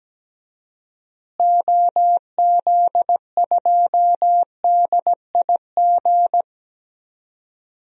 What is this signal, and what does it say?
Morse code, heard as a keyed tone, about 17 wpm, 700 Hz. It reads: OZ2DIG
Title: DXCC-100 Call Signs - 6 Characters - Repeated 2x - 17wpm